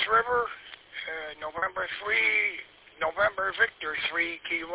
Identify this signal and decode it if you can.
Emergency Alert System (SAME) tones, heard as harsh buzzing bursts, with speech. It is heard from 0.9 to 3.7 s.